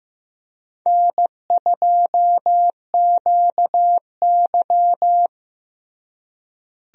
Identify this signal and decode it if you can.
Morse code, heard as a keyed tone, about 15 wpm, 700 Hz. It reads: N2QY
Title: USA Call Signs - 4 Characters - Repeated 2x - 15wpm